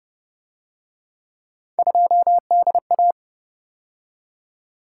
Morse code, heard as a keyed tone, about 31 wpm, 700 Hz. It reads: 2DA
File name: Speed-Racing - Three Letter-Number Combinations - 25wpm